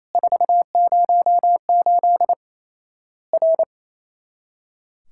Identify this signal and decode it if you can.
Morse code, heard as a keyed tone, about 27 wpm, 700 Hz. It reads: 408 R
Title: Worked All Europe DX Contest - QTC Exchange - 28wpm